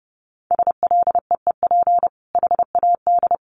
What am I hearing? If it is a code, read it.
Morse code, heard as a keyed tone, about 30 wpm, 700 Hz. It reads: SLEEP HAD
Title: Sentences from Top 1100 Words 30wpm